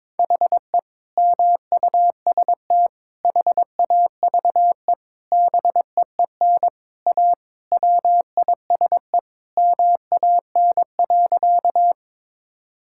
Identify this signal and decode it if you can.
Morse code, heard as a keyed tone, about 22 wpm, 700 Hz. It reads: HE MUST HAVE BEEN A WISE MAN.
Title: Sentences from Top 1600 Words 22wpm